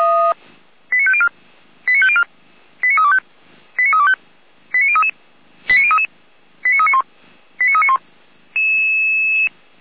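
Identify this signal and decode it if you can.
Five-tone selective call (ZVEI), heard as quick runs of five tones, tones of about 70 ms. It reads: 89504 89504 89325 89325 89033 89033 89391 89391
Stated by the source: ZVEI five-tone call 89140 received on SDR, logged by feuerwehr-orlamuende fire brigade.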